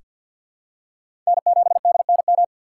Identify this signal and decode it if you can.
Morse code, heard as a keyed tone, about 46 wpm, 700 Hz. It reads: N7DNM
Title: USA Call Signs - 5 Characters - 3x Character Spacing - 50wpm